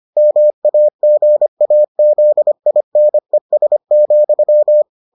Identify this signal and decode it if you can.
Morse code, heard as a keyed tone, about 25 wpm, 600 Hz. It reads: MAGAZINES,